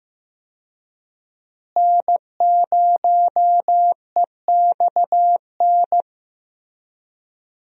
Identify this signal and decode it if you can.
Morse code, heard as a keyed tone, about 15 wpm, 700 Hz. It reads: N0EXN
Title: USA Call Signs - 5 Characters - Repeated 2x - 15wpm